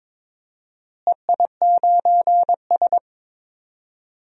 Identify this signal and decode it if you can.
Morse code, heard as a keyed tone, about 22 wpm, 700 Hz. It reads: EI9S
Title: DXCC-100 Call Signs - 4 Characters - Repeated 3x - 22wpm